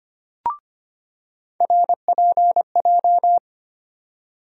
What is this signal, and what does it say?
Morse code, heard as a keyed tone, about 25 wpm, 700 Hz. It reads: RPJ
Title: USA Call Sign Postfixes - 3 Letters 25wpm